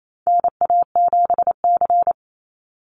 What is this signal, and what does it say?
Morse code, heard as a keyed tone, about 28 wpm, 700 Hz. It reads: NA7C
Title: Call Signs Repeated 2x - 28wpm